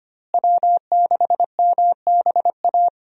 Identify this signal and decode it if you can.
Morse code, heard as a keyed tone, about 25 wpm, 700 Hz. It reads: W6MBA
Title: USA Call Signs - 5 Characters - Repeated 2x - 25wpm